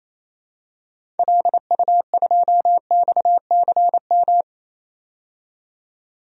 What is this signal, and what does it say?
Morse code, heard as a keyed tone, about 28 wpm, 700 Hz. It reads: LU2XCM
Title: DXCC-100 Call Signs - 6 Characters - 28wpm